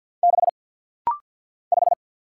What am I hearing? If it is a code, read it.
Morse code, heard as a keyed tone, about 48 wpm, 700 Hz. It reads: 6 5
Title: Single Letter-Number - Rapid-Fire 50wpm